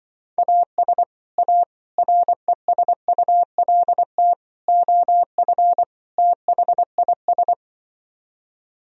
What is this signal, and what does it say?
Morse code, heard as a keyed tone, about 24 wpm, 700 Hz. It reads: AS A RESULT OF THIS
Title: Speed-Racing - Sets of 5 Words 20wpm